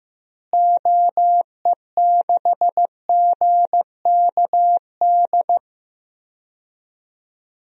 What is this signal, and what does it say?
Morse code, heard as a keyed tone, about 15 wpm, 700 Hz. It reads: OE6GKD
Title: DXCC-100 Call Signs - 6 Characters - Repeated 2x - 15wpm